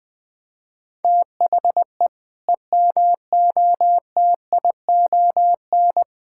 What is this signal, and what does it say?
Morse code, heard as a keyed tone, about 20 wpm, 700 Hz. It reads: THE EMOTION
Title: Sentences from Top 1600 Words 20wpm